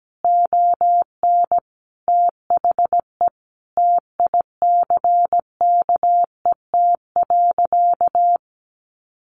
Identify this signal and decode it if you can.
Morse code, heard as a keyed tone, about 17 wpm, 700 Hz. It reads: ON THE TICKET.
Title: Sentences from Top 1400 Words 17wpm